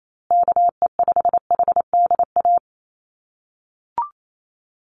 Morse code, heard as a keyed tone, about 28 wpm, 700 Hz. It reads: KE5HDA
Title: USA Call Signs Repeated 2x 28wpm